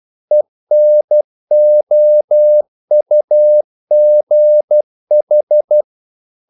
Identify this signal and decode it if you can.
Morse code, heard as a keyed tone, about 12 wpm, 600 Hz. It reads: ENOUGH
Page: English quotations in CW (telegraphy - morse)